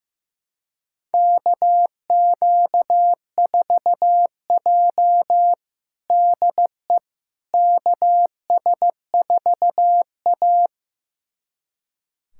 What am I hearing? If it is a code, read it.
Morse code, heard as a keyed tone, about 15 wpm, 700 Hz. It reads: KQ4J DE KS4A
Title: USA Callsign DE Callsign - 4 Characters - 15wpm